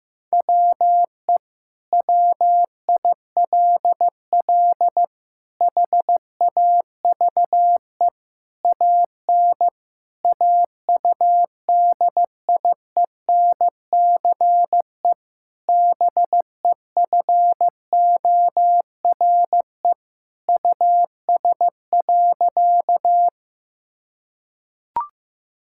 Morse code, heard as a keyed tone, about 15 wpm, 700 Hz. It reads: WE WILL HAVE AN AUDIENCE BEFORE US.
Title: Sentences from Top 1000 Words - Review Part 5 - 15wpm